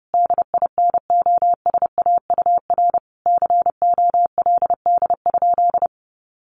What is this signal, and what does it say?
Morse code, heard as a keyed tone, about 28 wpm, 700 Hz. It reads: DINOSAUR COLD?
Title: A Sound Of Thunder 30wpm